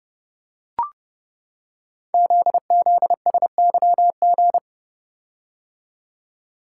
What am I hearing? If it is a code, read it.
Morse code, heard as a keyed tone, about 30 wpm, 700 Hz. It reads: ZZSYG
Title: Code Groups - 5 Letters - 30wpm